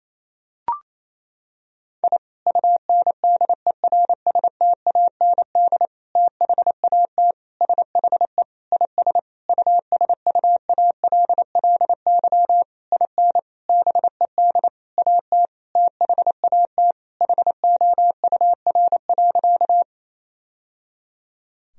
Morse code, heard as a keyed tone, about 28 wpm, 700 Hz. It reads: I UNDERSTAND THAT SHE IS USUALLY IN BED AT THAT HOUR.
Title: Sentences from Top 1000 Words - Review Part 3 - 28wpm